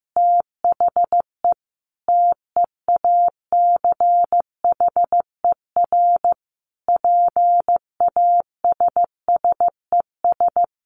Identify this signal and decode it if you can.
Morse code, heard as a keyed tone, about 15 wpm, 700 Hz. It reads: THE TEACHER PASSES
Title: Additional Easy Phrases 15wpm